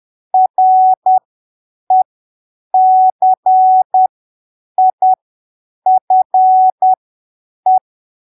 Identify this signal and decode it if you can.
Morse code, heard as a keyed tone, about 10 wpm, spaced as for 7 wpm, 750 Hz. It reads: RECIFE
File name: arquivos / texto / texto 16 10ppm